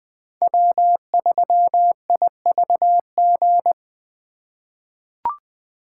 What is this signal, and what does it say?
Morse code, heard as a keyed tone, about 20 wpm, 700 Hz. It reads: W3IVG
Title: USA Call Signs - 2x Character Spacing - 20wpm